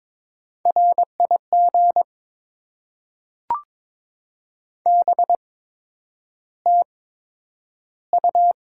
Morse code, heard as a keyed tone, about 22 wpm, spaced as for 9 wpm, 700 Hz. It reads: RIG BTU
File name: Top 20 QSO Elements - 8x Character Spacing - 22wpm